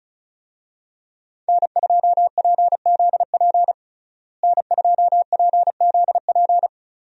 Morse code, heard as a keyed tone, about 36 wpm, 700 Hz. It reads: N2PZP N2PZP
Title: USA Call Signs - 5 Characters - Repeated 2x - 35wpm